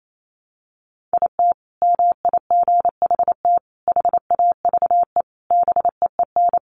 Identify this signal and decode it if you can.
Morse code, heard as a keyed tone, about 28 wpm, 700 Hz. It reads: IT MIGHT HAVE BEEN
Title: Sentences from Top 1200 Words 28wpm